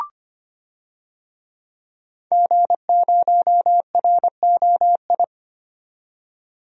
Morse code, heard as a keyed tone, about 25 wpm, 700 Hz. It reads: G0ROI
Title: DXCC-100 Call Signs - 5 Characters - 25wpm